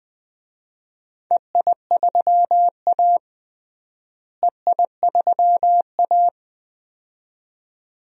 Morse code, heard as a keyed tone, about 20 wpm, 700 Hz. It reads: EI3A EI3A
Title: DXCC-100 Call Signs - 4 Characters - Repeated 2x - 20wpm